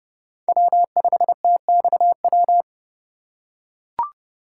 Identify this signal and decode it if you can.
Morse code, heard as a keyed tone, about 30 wpm, 700 Hz. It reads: W5TXW
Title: USA Call Signs - 5 Characters - 8x Character Spacing - 30wpm